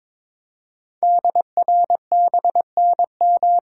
Morse code, heard as a keyed tone, about 23 wpm, 700 Hz. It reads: DRBNM
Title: Code Groups - 5 Letters - 22wpm